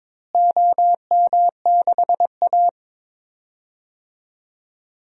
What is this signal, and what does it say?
Morse code, heard as a keyed tone, about 22 wpm, 700 Hz. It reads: OM6A
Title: DXCC-100 Call Signs - 4 Characters - Repeated 3x - 22wpm